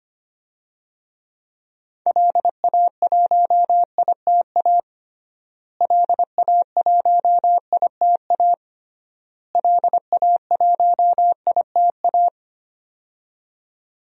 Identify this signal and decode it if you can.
Morse code, heard as a keyed tone, about 25 wpm, 700 Hz. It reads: LA1ITA LA1ITA LA1ITA
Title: DXCC-100 Call Signs - 6 Characters - Repeated 3x - 25wpm